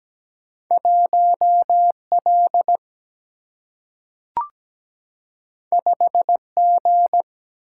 Morse code, heard as a keyed tone, about 17 wpm, 700 Hz. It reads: 1L 5G
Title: Single-Letter Single-Number 17wpm